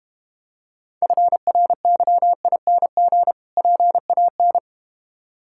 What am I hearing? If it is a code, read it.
Morse code, heard as a keyed tone, about 31 wpm, 700 Hz. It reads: FRYING PAN
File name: Difficult Words - Speed-Racing with Hints - 25wpm